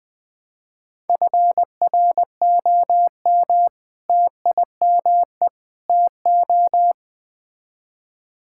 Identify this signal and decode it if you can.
Morse code, heard as a keyed tone, about 20 wpm, 700 Hz. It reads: FROM TIME TO
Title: Build A Phrase - 4-Word Phrases - 20wpm